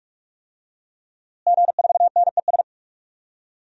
Morse code, heard as a keyed tone, about 41 wpm, 700 Hz. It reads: G4NES
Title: DXCC-100 Call Signs - 5 Characters - Repeated 2x - 45wpm